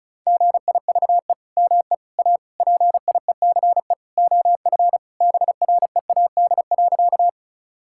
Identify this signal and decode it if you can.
Morse code, heard as a keyed tone, about 34 wpm, 700 Hz. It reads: GIVE ME A PIECE OF BREAD.
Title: Sentences from Top 1600 Words 35wpm